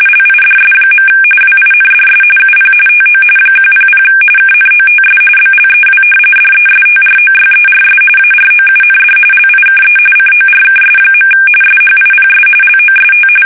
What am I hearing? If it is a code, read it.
RTTY, about 45.45 baud. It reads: LIDS. WINDS UP TO 100 KNOTS BLEW DESTROYERS AROUND LIKE CORKS. SHIPS LOST
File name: fsk850 / battleshipnj